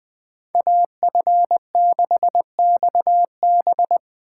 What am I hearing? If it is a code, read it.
Morse code, heard as a keyed tone, about 20 wpm, 700 Hz. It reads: AF6XB
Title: USA Call Signs - 5 Characters - 8x Character Spacing - 20wpm